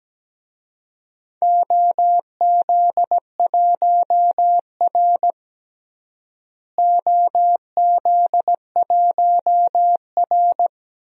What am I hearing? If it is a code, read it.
Morse code, heard as a keyed tone, about 17 wpm, 700 Hz. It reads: OZ1R OZ1R
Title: DXCC-100 Call Signs - 4 Characters - Repeated 3x - 17wpm